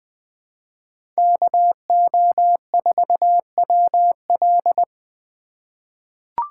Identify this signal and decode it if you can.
Morse code, heard as a keyed tone, about 20 wpm, 700 Hz. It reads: KO4WL
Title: USA Call Signs - 5 Characters - 3x Character Spacing - 20wpm